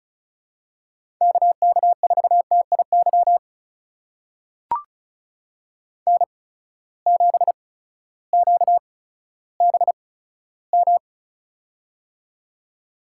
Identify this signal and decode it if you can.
Morse code, heard as a keyed tone, about 36 wpm, spaced as for 16 wpm, 700 Hz. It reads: KK4TIY N7QBM
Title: USA Call Signs - 8x Character Spacing - 35wpm